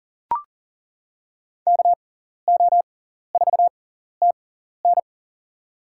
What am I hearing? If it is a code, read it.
Morse code, heard as a keyed tone, about 40 wpm, spaced as for 14 wpm, 700 Hz. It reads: KO4TN
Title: USA Call Signs - 6x Character Spacing - 40wpm